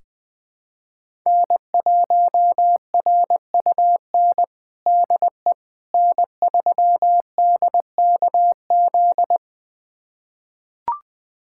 Morse code, heard as a keyed tone, about 21 wpm, 700 Hz. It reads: N1RUN DE N3DKZ
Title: USA Callsign DE Callsign - 5 Characters - 20wpm